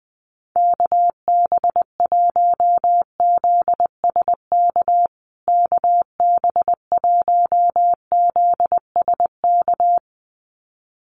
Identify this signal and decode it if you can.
Morse code, heard as a keyed tone, about 20 wpm, 700 Hz. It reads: KB1ZSK KB1ZSK
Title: USA Call Signs Repeated 2x 20wpm